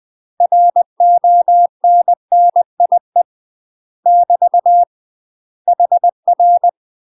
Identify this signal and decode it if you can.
Morse code, heard as a keyed tone, about 20 wpm, 700 Hz. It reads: RONNIE <BT> HR
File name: UK MASTER QSOs TRAINING / 20WPM / UK MASTER QSOs 0015